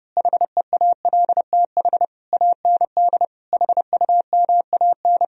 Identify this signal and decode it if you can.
Morse code, heard as a keyed tone, about 29 wpm, 700 Hz. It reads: HEALTH AND HUMAN
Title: Speed-Racing - Sets of 5 Words 30wpm